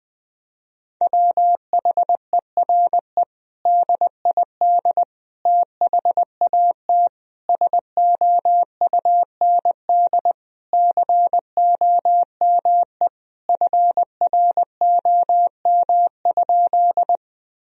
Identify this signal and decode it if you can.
Morse code, heard as a keyed tone, about 20 wpm, 700 Hz. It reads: WHERE DID THAT SOUND COME FROM?
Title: Sentences from Top 1000 Words - Review Part 1 - 20wpm